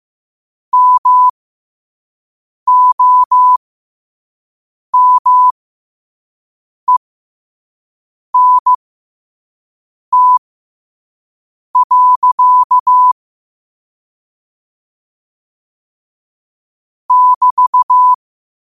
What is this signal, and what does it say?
Morse code, heard as a keyed tone, about 15 wpm, spaced as for 5 wpm, 1000 Hz. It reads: MOMENT. <BT>
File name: qotdfiles / 5wpm